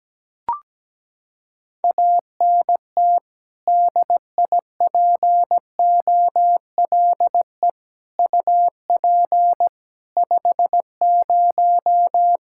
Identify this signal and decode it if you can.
Morse code, heard as a keyed tone, about 17 wpm, 700 Hz. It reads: ANT DIPOLE UP 50
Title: Practice QSO - Exchange 2 - Info - 17wpm